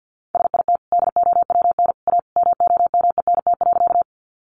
Morse code, heard as a keyed tone, about 48 wpm, 700 Hz. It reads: SIT DOWN A MOMENT.
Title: Sentences from Top 1000 Words 50wpm